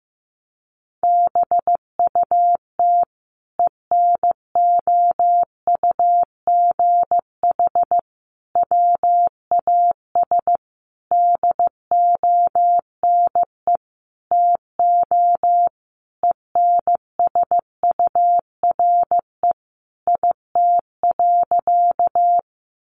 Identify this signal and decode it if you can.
Morse code, heard as a keyed tone, about 15 wpm, 700 Hz. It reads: BUT ENOUGH WAS DONE TO ENSURE IT.
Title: Sentences from Top 1100 Words 15wpm